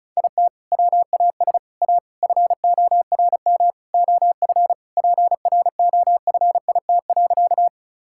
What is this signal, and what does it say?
Morse code, heard as a keyed tone, about 36 wpm, 700 Hz. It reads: IT WAS A FORM OF PROFIT.
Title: Sentences from Top 1000 Words - Review Part 5 - 35wpm